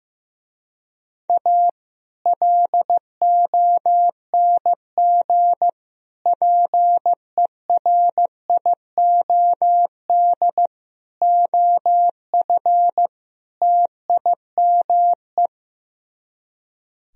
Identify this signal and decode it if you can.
Morse code, heard as a keyed tone, about 15 wpm, 700 Hz. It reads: A LONG PERIOD OF TIME
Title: Speed-Racing - Sets of 5 Words 15wpm